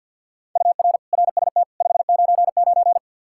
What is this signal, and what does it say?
Morse code, heard as a keyed tone, about 52 wpm, 700 Hz. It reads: UR RST 599
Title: Practice QSO - Exchange 1 - Beginning - 50wpm